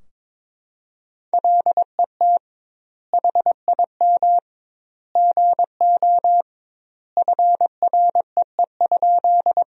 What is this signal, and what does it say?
Morse code, heard as a keyed tone, about 21 wpm, 700 Hz. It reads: LET HIM GO FREE?
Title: 2x Word-Spacing - Sentences from Top 500 Words - 22wpm